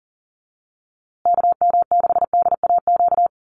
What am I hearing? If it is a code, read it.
Morse code, heard as a keyed tone, about 40 wpm, 700 Hz. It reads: KM6DAQ
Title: USA Call Signs Repeated 3x 40wpm